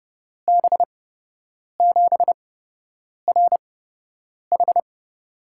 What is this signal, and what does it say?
Morse code, heard as a keyed tone, about 30 wpm, spaced as for 8 wpm, 700 Hz. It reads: B7RH
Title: USA Call Signs - 8x Character Spacing - 30wpm